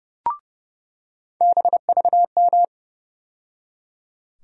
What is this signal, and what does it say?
Morse code, heard as a keyed tone, about 30 wpm, 700 Hz. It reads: BVM
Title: USA Call Sign Postfixes - 3 Letters 30wpm